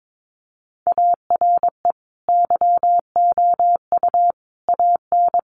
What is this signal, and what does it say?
Morse code, heard as a keyed tone, about 22 wpm, 700 Hz. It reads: ARE YOU AN
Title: Sentences from Top 1200 Words 22wpm